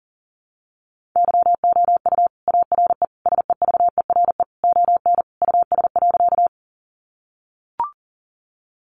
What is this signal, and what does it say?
Morse code, heard as a keyed tone, about 40 wpm, 700 Hz. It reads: YOU ARE SEVERE ON US.
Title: Sentences from Top 1400 Words 40wpm